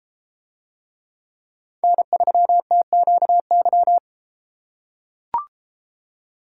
Morse code, heard as a keyed tone, about 32 wpm, 700 Hz. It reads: N3TQY